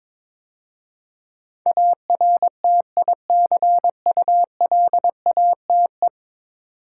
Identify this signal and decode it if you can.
Morse code, heard as a keyed tone, about 22 wpm, 700 Hz. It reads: ARTICULATE
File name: practice / 22wpm / 10 Letter Words - First 3 Letters Given - 22wpm